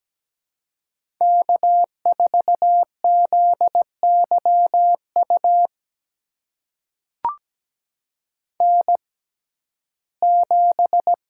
Morse code, heard as a keyed tone, about 17 wpm, spaced as for 12 wpm, 700 Hz. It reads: K4ZYU N7
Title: USA Call Signs - 5 Characters - 6x Character Spacing - 17wpm